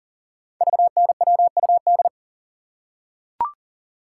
Morse code, heard as a keyed tone, about 41 wpm, 690 Hz. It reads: VNWUD
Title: Code Groups - 5 Letters - 40wpm